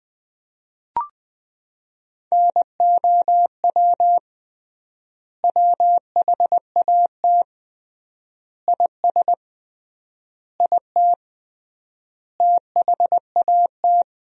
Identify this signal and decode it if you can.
Morse code, heard as a keyed tone, about 20 wpm, 700 Hz. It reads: NOW WHAT IS IT THAT
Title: 3x Word-Spacing - Sentences from Top 200 Words - 20wpm